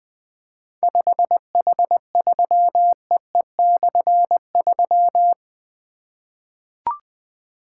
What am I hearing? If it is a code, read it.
Morse code, heard as a keyed tone, about 20 wpm, 700 Hz. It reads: 5H3EE/3
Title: Difficult Call Signs 20wpm